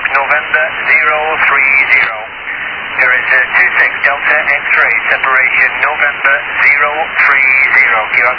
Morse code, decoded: none